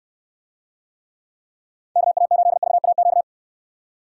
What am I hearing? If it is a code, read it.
Morse code, heard as a keyed tone, about 69 wpm, 700 Hz. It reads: KA8VAQ